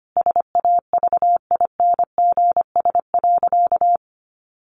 Morse code, heard as a keyed tone, about 25 wpm, 700 Hz. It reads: SAVINGS.
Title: Sentences from Top 1400 Words 25wpm